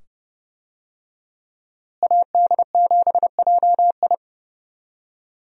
Morse code, heard as a keyed tone, about 29 wpm, 700 Hz. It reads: AD7JI
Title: USA Call Signs - 5 Characters - Repeated 2x - 30wpm